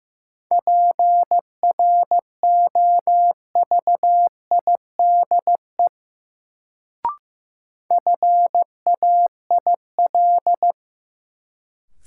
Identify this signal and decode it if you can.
Morse code, heard as a keyed tone, about 15 wpm, 700 Hz. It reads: PROVIDE FAIL
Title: Sentences from Top 1100 Words - New Word Drill Alt - 15wpm